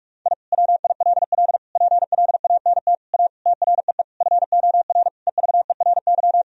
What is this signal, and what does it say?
Morse code, heard as a keyed tone, about 46 wpm, 700 Hz. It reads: I WILL PLANT A TREE FOR EVERY
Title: Sentences from Top 1000 Words - Review Part 1 - 45wpm